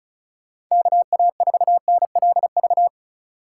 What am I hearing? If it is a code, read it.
Morse code, heard as a keyed tone, about 32 wpm, 700 Hz. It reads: KA4NLV
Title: Speed Racing - USA 2x3 Call Signs 30wpm